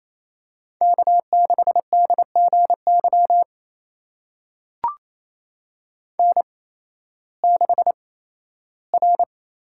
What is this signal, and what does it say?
Morse code, heard as a keyed tone, about 27 wpm, spaced as for 15 wpm, 700 Hz. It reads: K6DGY N6R
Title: USA Call Signs - 5 Characters - 8x Character Spacing - 28wpm